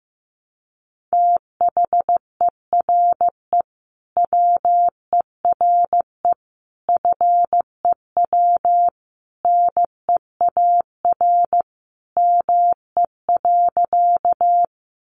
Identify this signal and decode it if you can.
Morse code, heard as a keyed tone, about 15 wpm, 700 Hz. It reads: THERE WERE FEW NEAR ME.
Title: Sentences from Top 300 Words 15wpm